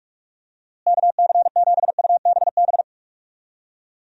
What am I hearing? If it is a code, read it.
Morse code, heard as a keyed tone, about 44 wpm, 700 Hz. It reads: KC7UBB
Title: USA Call Signs - 6 Characters - Repeated 2x - 45wpm